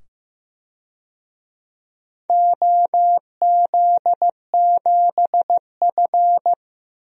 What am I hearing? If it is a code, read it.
Morse code, heard as a keyed tone, about 15 wpm, 700 Hz. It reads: OZ7F
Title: DXCC-100 Call Signs - 4 Characters - 15wpm